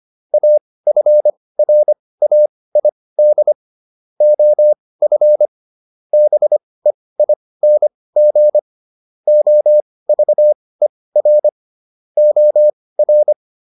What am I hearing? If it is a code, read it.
Morse code, heard as a keyed tone, about 25 wpm, spaced as for 18 wpm, 600 Hz. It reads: AFRAID OF BEING OVER OR
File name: SS108 18